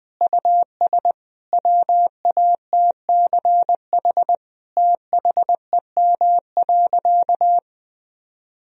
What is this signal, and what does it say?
Morse code, heard as a keyed tone, about 20 wpm, 700 Hz. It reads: US WATCH THEM.
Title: Sentences from Top 1000 Words - Review Part 1 - 20wpm